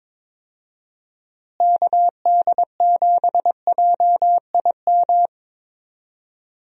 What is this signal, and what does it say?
Morse code, heard as a keyed tone, about 22 wpm, 700 Hz. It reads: KD7JIM
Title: USA Call Signs - 6 Characters - Repeated 3x - 22wpm